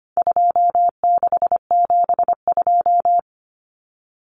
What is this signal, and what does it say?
Morse code, heard as a keyed tone, about 25 wpm, 700 Hz. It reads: 2672
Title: Numbers - 4-Digit 25wpm